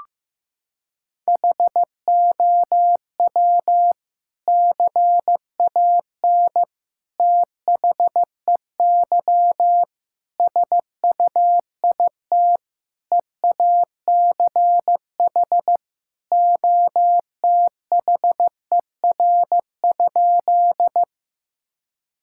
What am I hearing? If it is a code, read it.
Morse code, heard as a keyed tone, about 15 wpm, 700 Hz. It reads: HOW CAN THEY SUIT EACH OTHER?
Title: Sentences from Top 1600 Words 15wpm